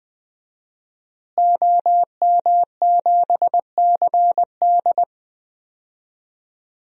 Morse code, heard as a keyed tone, about 20 wpm, 700 Hz. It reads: OM7CD
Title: DXCC-100 Call Signs - 5 Characters - Repeated 3x - 20wpm